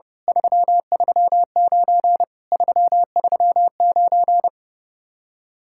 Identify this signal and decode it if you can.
Morse code, heard as a keyed tone, about 30 wpm, 700 Hz. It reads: 339 339
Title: Speed-Racing Common QSO Phrases - 30wpm